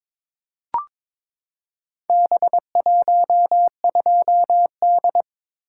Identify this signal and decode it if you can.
Morse code, heard as a keyed tone, about 21 wpm, 700 Hz. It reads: B12D
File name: practice / 22wpm / Four Letter-Number Combinations - 22wpm